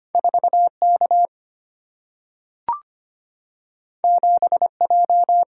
Morse code, heard as a keyed tone, about 25 wpm, 700 Hz. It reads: 4 K 7 J
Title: Single-Letter Single-Number 25wpm